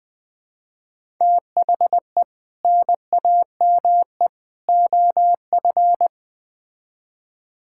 Morse code, heard as a keyed tone, about 20 wpm, 700 Hz. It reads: THE NAME OF
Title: Build A Phrase - 3-Word Phrases - 20wpm